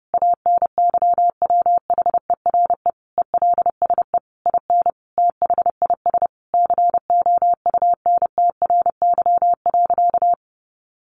Morse code, heard as a keyed tone, about 30 wpm, 700 Hz. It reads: ANYWHERE ELSE IN THIS COUNTRY.
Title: Sentences from Top 1400 Words 30wpm